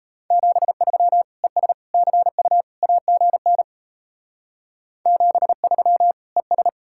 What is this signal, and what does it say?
Morse code, heard as a keyed tone, about 34 wpm, 700 Hz. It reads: 73 ES CU AGN 73 ES
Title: Speed-Racing Common QSO Phrases - 28wpm